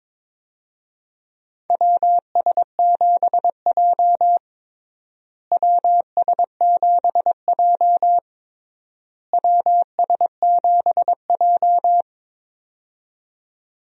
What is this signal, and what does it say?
Morse code, heard as a keyed tone, about 21 wpm, 700 Hz. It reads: WS7J WS7J WS7J
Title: USA Call Signs - 4 Characters - Repeated 3x - 22wpm